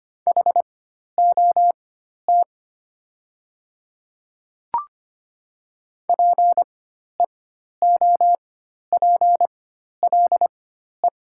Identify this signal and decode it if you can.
Morse code, heard as a keyed tone, about 25 wpm, spaced as for 12 wpm, 690 Hz. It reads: HOT PEOPLE
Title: 4x Character Spacing - Top 200 Words - 25wpm